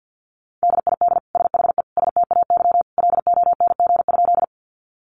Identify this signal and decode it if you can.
Morse code, heard as a keyed tone, about 50 wpm, 700 Hz. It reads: DID SHE STAY LONG?
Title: Sentences from Top 500 Words 50wpm